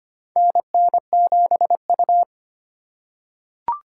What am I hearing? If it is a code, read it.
Morse code, heard as a keyed tone, about 25 wpm, 700 Hz. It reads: NN7U
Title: USA Call Signs - 4 Characters - 4x Character Spacing - 25wpm